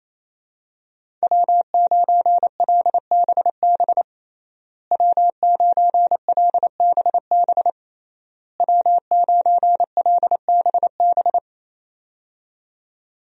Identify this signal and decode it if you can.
Morse code, heard as a keyed tone, about 27 wpm, 700 Hz. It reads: W9LBB W9LBB W9LBB
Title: USA Call Signs - 5 Characters - Repeated 3x - 28wpm